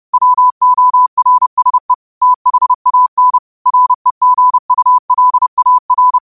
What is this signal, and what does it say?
Morse code, heard as a keyed tone, about 30 wpm, 1000 Hz. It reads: WORSE THAN REGULAR
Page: Quotes for Fri, 15 Aug 2025 in Morse Code at 30 words per minute.